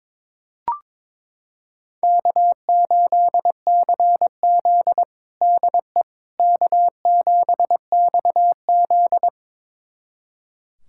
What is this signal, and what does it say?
Morse code, heard as a keyed tone, about 22 wpm, 700 Hz. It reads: K8CZ DE K7XZ
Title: USA Callsign DE Callsign - 4 Characters - 22wpm